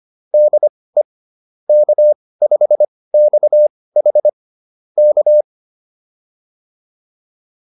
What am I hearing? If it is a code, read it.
Morse code, heard as a keyed tone, about 25 wpm, spaced as for 18 wpm, 600 Hz. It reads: DE K5XH K